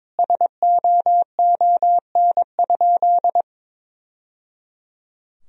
Morse code, heard as a keyed tone, about 22 wpm, 700 Hz. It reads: SOON?
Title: Sentences from Top 1000 Words - Review Part 1 - 22wpm